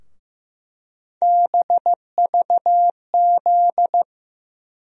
Morse code, heard as a keyed tone, about 15 wpm, 700 Hz. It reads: BVZ
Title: USA Call Sign Postfixes - 3 Letters 15wpm